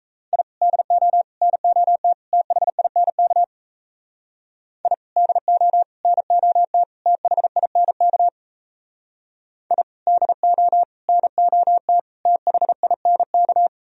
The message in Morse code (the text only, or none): I DO NOT THINK I DO NOT THINK I DO NOT THINK